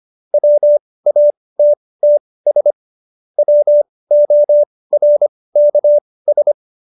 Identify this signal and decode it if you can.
Morse code, heard as a keyed tone, about 25 wpm, spaced as for 18 wpm, 590 Hz. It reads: WATTS WORKS
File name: qso222 18